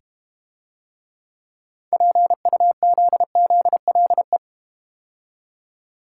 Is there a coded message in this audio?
Yes: Morse code from 1.9 to 4.4 s, heard as a keyed tone, about 32 wpm, 700 Hz. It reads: PUZZLE